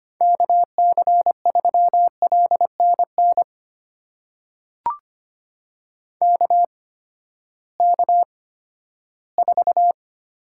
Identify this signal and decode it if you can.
Morse code, heard as a keyed tone, about 25 wpm, spaced as for 14 wpm, 700 Hz. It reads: KC3LNN KK4
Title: USA Call Signs - 6 Characters - 8x Character Spacing - 25wpm